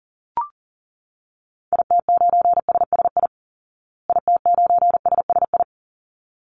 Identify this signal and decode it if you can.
Morse code, heard as a keyed tone, about 40 wpm, 700 Hz. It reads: IT9SSI IT9SSI
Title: Call Signs Repeated 3x - 40wpm